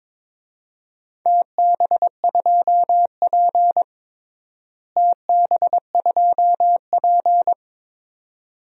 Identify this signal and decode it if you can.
Morse code, heard as a keyed tone, about 22 wpm, 700 Hz. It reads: TB2P TB2P
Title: DXCC-100 Call Signs - 4 Characters - Repeated 2x - 22wpm